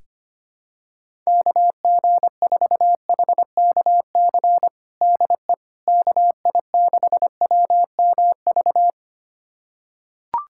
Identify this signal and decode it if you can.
Morse code, heard as a keyed tone, about 25 wpm, 700 Hz. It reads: KG4HKC DE KI6WMV
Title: USA Callsign DE Callsign 25wpm